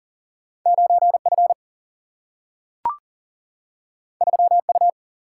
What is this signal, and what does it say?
Morse code, heard as a keyed tone, about 40 wpm, 700 Hz. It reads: 9F 3U
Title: Single-Letter Single-Number 40wpm